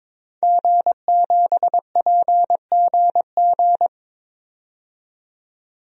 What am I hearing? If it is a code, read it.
Morse code, heard as a keyed tone, about 23 wpm, 700 Hz. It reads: G7PGG
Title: DXCC-100 Call Signs - 5 Characters - 22wpm